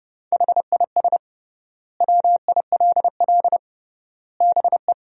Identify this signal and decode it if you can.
Morse code, heard as a keyed tone, about 32 wpm, 700 Hz. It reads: HIS WILL BE
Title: 3x Word-Spacing - Sentences from Top 500 Words - 30wpm